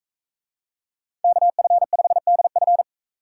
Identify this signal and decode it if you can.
Morse code, heard as a keyed tone, about 41 wpm, 700 Hz. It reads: KF5DF